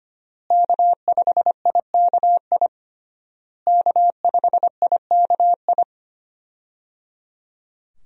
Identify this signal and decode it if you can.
Morse code, heard as a keyed tone, about 25 wpm, 700 Hz. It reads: K5IKI K5IKI
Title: USA Call Signs - 5 Characters - Repeated 3x - 25wpm